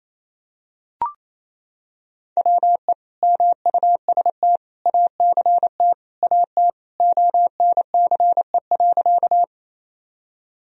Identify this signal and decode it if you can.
Morse code, heard as a keyed tone, about 28 wpm, 700 Hz. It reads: WE MUST ACT AT ONCE.
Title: Sentences from Top 1000 Words - Review Part 1 - 28wpm